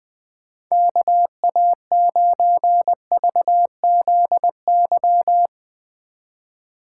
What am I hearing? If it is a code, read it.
Morse code, heard as a keyed tone, about 20 wpm, 700 Hz. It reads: KA9VZY
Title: Speed Racing - USA 2x3 Call Signs 20wpm